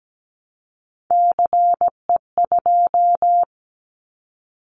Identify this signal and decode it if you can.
Morse code, heard as a keyed tone, about 17 wpm, 700 Hz. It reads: CE2
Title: Callsign Prefixes for Top 100 DXCC - 17wpm